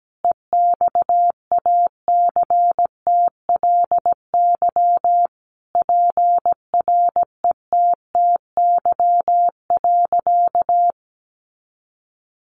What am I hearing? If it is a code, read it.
Morse code, heard as a keyed tone, about 17 wpm, 700 Hz. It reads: EXACTLY PRETTY.
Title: Sentences from Top 1000 Words 17wpm